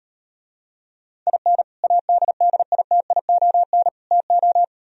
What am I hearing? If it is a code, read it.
Morse code, heard as a keyed tone, about 38 wpm, 700 Hz. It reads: IN ADDITION TO